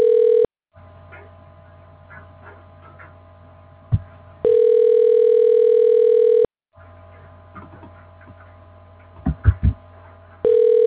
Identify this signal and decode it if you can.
Ringback tone (North America), 440 + 480 Hz.